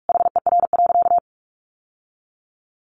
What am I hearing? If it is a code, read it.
Morse code, heard as a keyed tone, about 44 wpm, 700 Hz. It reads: HER.
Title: Sentences from Top 1500 Words 45wpm